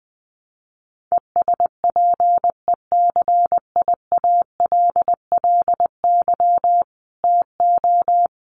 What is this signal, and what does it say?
Morse code, heard as a keyed tone, about 20 wpm, 700 Hz. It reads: ESPECIALLY TO
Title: Sentences from Top 1500 Words 20wpm